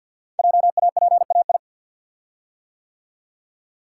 Morse code, heard as a keyed tone, about 50 wpm, 700 Hz. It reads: JAPAI
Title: Code Groups - 5 Letter:Numbers - 50wpm